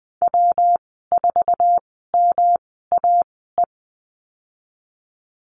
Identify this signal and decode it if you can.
Morse code, heard as a keyed tone, about 20 wpm, spaced as for 14 wpm, 700 Hz. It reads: W4MAE
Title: 140 Lesson 23g - Letter f - Call Signs 20wpm